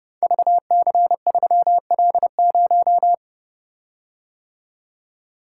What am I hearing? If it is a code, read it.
Morse code, heard as a keyed tone, about 30 wpm, 700 Hz. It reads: VC3L0
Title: Code Groups - 5 Letter:Numbers - 30wpm